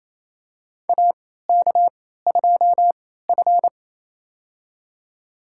Morse code, heard as a keyed tone, about 28 wpm, spaced as for 16 wpm, 700 Hz. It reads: AK2F
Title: USA Call Signs - 3x Character Spacing - 28wpm